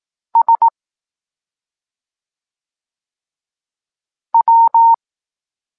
Morse code, heard as a keyed tone, about 18 wpm, 920 Hz. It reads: S W
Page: Tekst 23 – Doskonalenie odbioru poznanych liter